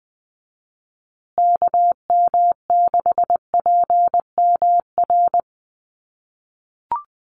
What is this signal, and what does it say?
Morse code, heard as a keyed tone, about 20 wpm, 700 Hz. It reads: KM6PMR
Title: USA Call Signs Repeated 2x 20wpm